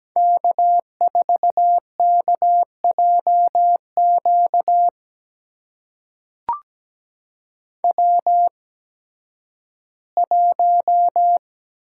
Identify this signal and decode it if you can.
Morse code, heard as a keyed tone, about 17 wpm, spaced as for 11 wpm, 700 Hz. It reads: K4KJQ W1
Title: USA Call Signs - 8x Character Spacing - 17wpm